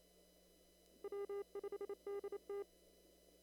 Morse code, heard as a keyed tone, about 28 wpm, 400 Hz. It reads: W5DT
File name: CW / kt5j wpx cw-op-k5pi